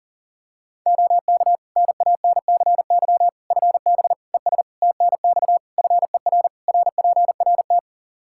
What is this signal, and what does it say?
Morse code, heard as a keyed tone, about 38 wpm, 700 Hz. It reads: OK NANCY FB ES TNX FER RPRT
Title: Practice QSO - Exchange 2 - Info - 40wpm